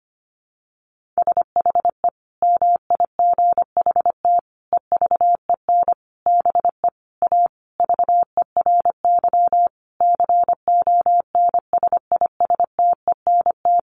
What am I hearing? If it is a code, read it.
Morse code, heard as a keyed tone, about 25 wpm, 700 Hz. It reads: SHE MIGHT EVEN BE A VERY CONSISTENT
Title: Sentences from Top 1500 Words 25wpm